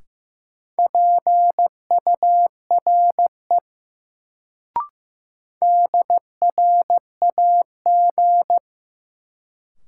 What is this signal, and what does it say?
Morse code, heard as a keyed tone, about 15 wpm, 700 Hz. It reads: PURE DRAG
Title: Sentences from Top 1500 Words - New Word Drill Alt - 15wpm